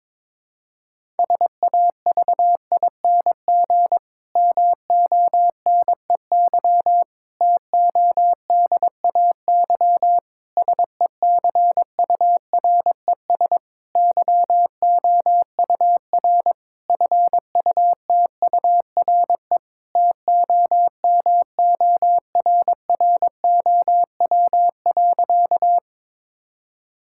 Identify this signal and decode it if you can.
Morse code, heard as a keyed tone, about 22 wpm, 700 Hz. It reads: SAVING MONEY TODAY SECURES YOUR FUTURE TOMORROW.
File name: Difficult Sentences with Hints - 22wpm